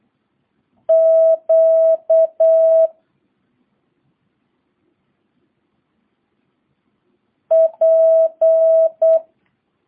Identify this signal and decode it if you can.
Morse code, heard as a keyed tone, about 8 wpm, 650 Hz. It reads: Q P